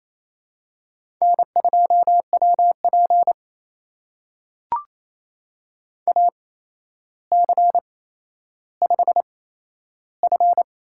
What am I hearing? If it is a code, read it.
Morse code, heard as a keyed tone, about 28 wpm, spaced as for 12 wpm, 700 Hz. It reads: N2WP AC5F
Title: USA Call Signs - 4 Characters - 8x Character Spacing - 28wpm